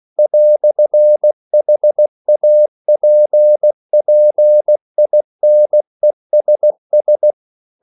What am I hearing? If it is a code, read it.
Morse code, heard as a keyed tone, about 16 wpm, 600 Hz. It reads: "HAPPINESS